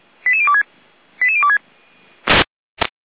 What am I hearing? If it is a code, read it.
Five-tone selective call (ZVEI), heard as quick runs of five tones, tones of about 70 ms. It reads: 88026 88026